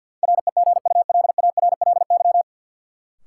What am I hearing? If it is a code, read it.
Morse code, heard as a keyed tone, about 52 wpm, 700 Hz. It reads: REGULARLY
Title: Sentences from Top 1400 Words - New Word Drill - 50wpm